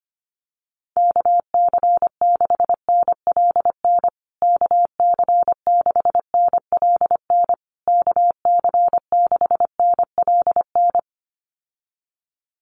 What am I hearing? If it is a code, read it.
Morse code, heard as a keyed tone, about 25 wpm, 700 Hz. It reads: KC6NLN KC6NLN KC6NLN
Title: USA Call Signs Repeated 3x 25wpm